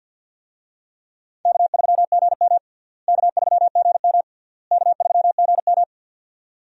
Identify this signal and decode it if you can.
Morse code, heard as a keyed tone, about 52 wpm, 700 Hz. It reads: K3GM K3GM K3GM
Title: USA Call Signs - 4 Characters - Repeated 3x - 50wpm